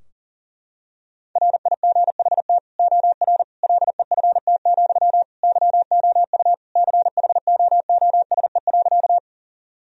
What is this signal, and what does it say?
Morse code, heard as a keyed tone, about 37 wpm, 700 Hz. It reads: RIGHT OR LEFT, YOU CHOOSE.
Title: Sentences from Top 1000 Words - Review Part 4 - 40wpm